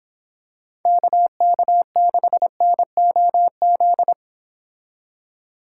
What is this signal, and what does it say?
Morse code, heard as a keyed tone, about 26 wpm, 700 Hz. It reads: KK6NOZ